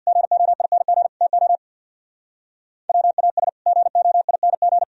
Morse code, heard as a keyed tone, about 62 wpm, 700 Hz. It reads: GOING TO WAS GOING
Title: Speed-Racing - Sets of 3 Words 50wpm